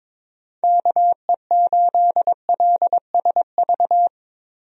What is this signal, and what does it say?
Morse code, heard as a keyed tone, about 22 wpm, 700 Hz. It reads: KE8LSV
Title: USA Call Signs - 6 Characters - 22wpm